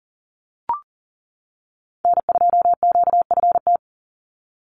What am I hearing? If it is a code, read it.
Morse code, heard as a keyed tone, about 40 wpm, 700 Hz. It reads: N2QFT
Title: USA Call Signs 40wpm